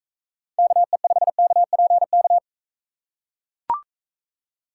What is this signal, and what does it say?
Morse code, heard as a keyed tone, about 42 wpm, 700 Hz. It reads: KE5KPK